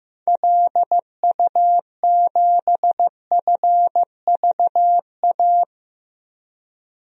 Morse code, heard as a keyed tone, about 15 wpm, 700 Hz. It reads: LU7FVA
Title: DXCC-100 Call Signs - 6 Characters - 15wpm